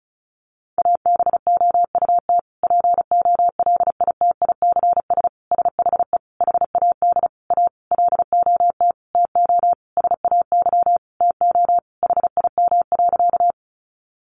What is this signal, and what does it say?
Morse code, heard as a keyed tone, about 34 wpm, 700 Hz. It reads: ABOUT POLITICS SHE HAD A LOT TO SAY TO HIM.
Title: Sentences from Top 1000 Words 35wpm